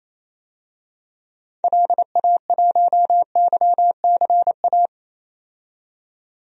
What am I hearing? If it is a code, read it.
Morse code, heard as a keyed tone, about 28 wpm, 700 Hz. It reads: LA1YCA
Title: DXCC-100 Call Signs - 6 Characters - 28wpm